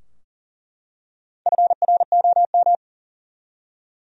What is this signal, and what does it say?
Morse code, heard as a keyed tone, about 40 wpm, 700 Hz. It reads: FROM
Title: Top 100 Words Repeated 40wpm